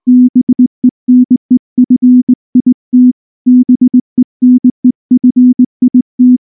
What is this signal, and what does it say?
Morse code, heard as a keyed tone, about 18 wpm, 250 Hz. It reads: BENEFIT BENEFIT
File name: morse / morse-training-20WPM-10